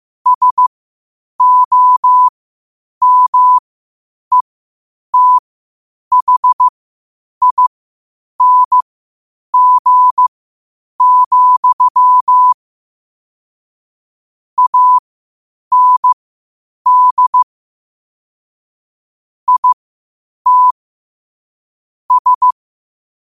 Morse code, heard as a keyed tone, about 15 wpm, spaced as for 8 wpm, 1000 Hz. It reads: SOMETHING, AND IT S
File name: 8wpm